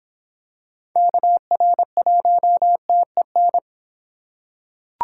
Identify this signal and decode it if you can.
Morse code, heard as a keyed tone, about 26 wpm, 700 Hz. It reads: KR1TEN